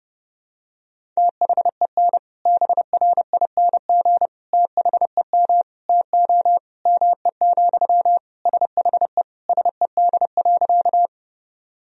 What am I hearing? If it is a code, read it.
Morse code, heard as a keyed tone, about 30 wpm, 700 Hz. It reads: THEN BRING THEM TO ME, SHE SED.
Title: Sentences from Top 500 Words and Abbreviations 30wpm